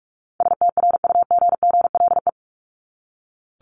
Morse code, heard as a keyed tone, about 44 wpm, 700 Hz. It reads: STRUGGLE
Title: Top 1751-2000 Words 45wpm